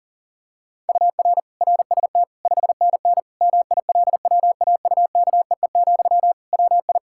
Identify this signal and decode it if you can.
Morse code, heard as a keyed tone, about 40 wpm, 700 Hz. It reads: UR RST 5NN MILWAUKEE, WI
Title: K1USN SST QSO Practice 40wpm